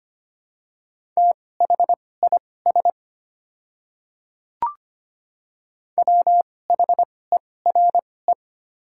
Morse code, heard as a keyed tone, about 25 wpm, spaced as for 18 wpm, 700 Hz. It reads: THIS WHERE
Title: 2x Character Spacing - Top 300 Words - 25wpm